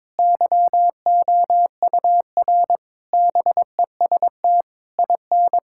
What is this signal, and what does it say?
Morse code, heard as a keyed tone, about 22 wpm, 700 Hz. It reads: YOUR BEST IN
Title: Sentences from Top 1000 Words - Review Part 5 - 22wpm